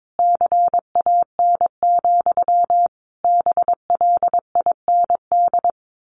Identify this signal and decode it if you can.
Morse code, heard as a keyed tone, about 22 wpm, 700 Hz. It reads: CAN, BLIND
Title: A Sound Of Thunder 22wpm